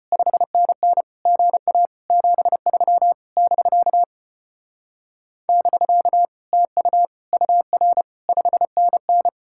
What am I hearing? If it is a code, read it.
Morse code, heard as a keyed tone, about 33 wpm, 700 Hz. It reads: 5NN GA 73 <BK> <BK> TU UR 5NN